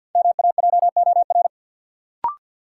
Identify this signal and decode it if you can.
Morse code, heard as a keyed tone, about 53 wpm, 700 Hz. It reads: MAJOR
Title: Sentences from Top 600 Words - New Word Drill - 50wpm